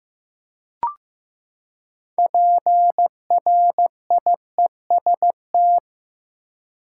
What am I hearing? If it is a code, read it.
Morse code, heard as a keyed tone, about 15 wpm, 700 Hz. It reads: PRIEST
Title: Six Letter Words - 15wpm